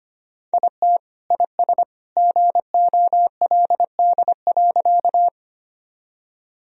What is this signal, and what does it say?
Morse code, heard as a keyed tone, about 25 wpm, 700 Hz. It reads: IT IS GOLD.
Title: Sentences from Top 1000 Words - Review Part 3 - 25wpm